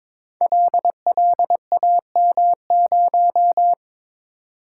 Morse code, heard as a keyed tone, about 22 wpm, 700 Hz. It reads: LLAM0
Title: Code Groups - 5 Letter:Numbers - 22wpm